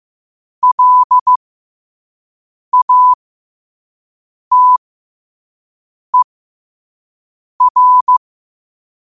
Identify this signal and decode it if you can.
Morse code, heard as a keyed tone, about 15 wpm, spaced as for 5 wpm, 1000 Hz. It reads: LATER
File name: qotdfiles / 5wpm